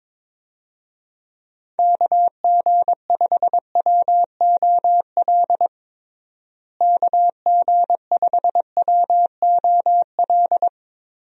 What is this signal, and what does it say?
Morse code, heard as a keyed tone, about 22 wpm, 700 Hz. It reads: KG5WOL KG5WOL
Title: USA Call Signs - 6 Characters - Repeated 3x - 22wpm